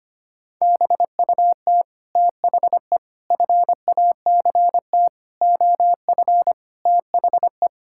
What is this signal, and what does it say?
Morse code, heard as a keyed tone, about 25 wpm, 700 Hz. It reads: BUT THE FACT OF THE
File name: practice / 25wpm / Sentences with 9-Word Phrases - 25wpm